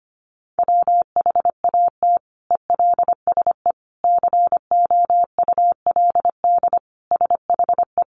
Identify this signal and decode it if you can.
Morse code, heard as a keyed tone, about 25 wpm, 700 Hz. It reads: WHAT ELSE COULD SHE
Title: Sentences from Top 800 Words 25wpm